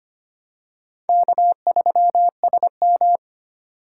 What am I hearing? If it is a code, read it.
Morse code, heard as a keyed tone, about 25 wpm, 700 Hz. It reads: K3SM
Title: USA Call Signs - 8x Character Spacing - 25wpm